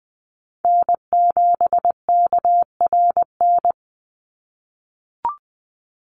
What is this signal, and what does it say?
Morse code, heard as a keyed tone, about 20 wpm, 700 Hz. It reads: N7KRN
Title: NA SOTA Chaser Callsigns 20wpm